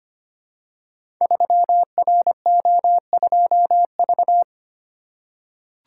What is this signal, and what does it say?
Morse code, heard as a keyed tone, about 25 wpm, 700 Hz. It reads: 3RO2V
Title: Code Groups - 5 Letter:Numbers - 25wpm